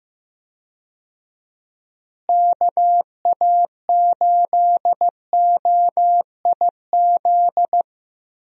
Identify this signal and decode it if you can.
Morse code, heard as a keyed tone, about 15 wpm, 700 Hz. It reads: KA8OIZ
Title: USA Call Signs - 6 Characters - Repeated 3x - 15wpm